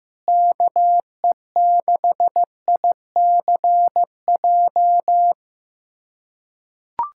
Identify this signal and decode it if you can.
Morse code, heard as a keyed tone, about 15 wpm, 700 Hz. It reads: KE6ICJ
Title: USA Callsign DE Callsign 15wpm